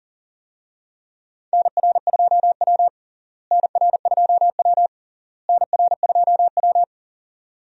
Morse code, heard as a keyed tone, about 37 wpm, 700 Hz. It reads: NR2W NR2W NR2W
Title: USA Call Signs - 4 Characters - Repeated 3x - 40wpm